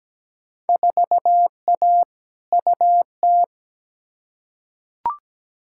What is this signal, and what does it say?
Morse code, heard as a keyed tone, about 17 wpm, 700 Hz. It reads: 4A UT
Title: Field Day Exchanges - 17wpm